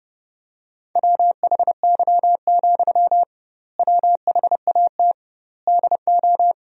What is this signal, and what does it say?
Morse code, heard as a keyed tone, about 30 wpm, 700 Hz. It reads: WHY, WHAT DO
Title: 2x Word-Spacing - Sentences from Top 200 Words - 30wpm